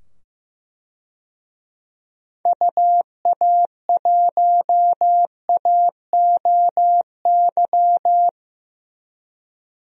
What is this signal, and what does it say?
Morse code, heard as a keyed tone, about 15 wpm, 700 Hz. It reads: UA1AOY
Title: DXCC-100 Call Signs - 6 Characters - 15wpm